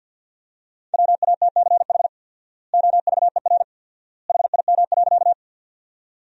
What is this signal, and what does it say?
Morse code, heard as a keyed tone, about 50 wpm, 700 Hz. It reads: WATCH OVER HIM.
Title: 4x Word-Spacing - Sentences from Top 300 Words - 50wpm